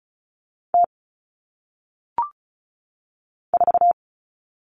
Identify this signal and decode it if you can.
Morse code, heard as a keyed tone, about 36 wpm, 700 Hz. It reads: T 4
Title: All Cut Numbers - 1-Digit 35wpm